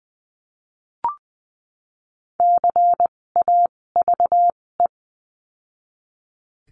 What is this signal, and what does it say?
Morse code, heard as a keyed tone, about 20 wpm, spaced as for 16 wpm, 700 Hz. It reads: CAVE
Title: 256 Lesson 42a - Words 12-20wpm